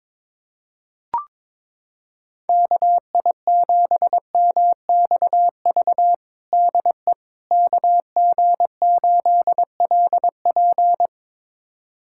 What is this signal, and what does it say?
Morse code, heard as a keyed tone, about 22 wpm, 700 Hz. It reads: KI7MXV DE KG8LP
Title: USA Callsign DE Callsign 22wpm